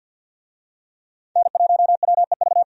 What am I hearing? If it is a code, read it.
Morse code, heard as a keyed tone, about 50 wpm, 700 Hz. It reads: N1WEV
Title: USA Call Signs - 5 Characters - 50wpm